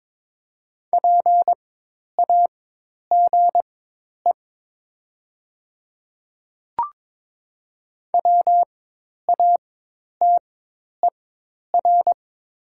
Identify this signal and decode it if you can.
Morse code, heard as a keyed tone, about 22 wpm, spaced as for 10 wpm, 700 Hz. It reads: PAGE WATER
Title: 4x Character Spacing - Top 500 Words - 22wpm